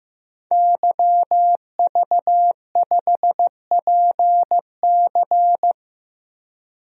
Morse code, heard as a keyed tone, about 15 wpm, 700 Hz. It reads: YV5PC
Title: DXCC-100 Call Signs - 5 Characters - Repeated 3x - 15wpm